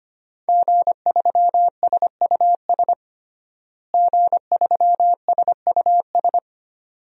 Morse code, heard as a keyed tone, about 25 wpm, 700 Hz. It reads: G3SUS G3SUS
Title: DXCC-100 Call Signs - 5 Characters - Repeated 2x - 25wpm